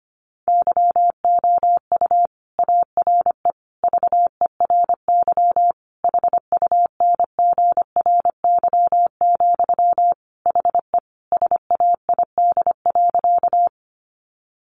Morse code, heard as a keyed tone, about 25 wpm, 700 Hz. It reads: YOU ARE VERY HUNGRY, HE SAID.
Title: Sentences from Top 1400 Words 25wpm